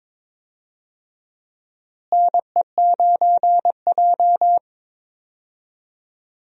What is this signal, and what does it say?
Morse code, heard as a keyed tone, about 22 wpm, 700 Hz. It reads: NE9J
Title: USA Call Signs - 4 Characters - Repeated 3x - 22wpm